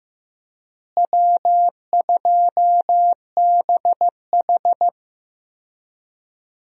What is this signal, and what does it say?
Morse code, heard as a keyed tone, about 15 wpm, 700 Hz. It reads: W2BH
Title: USA Call Signs - 4 Characters - Repeated 3x - 15wpm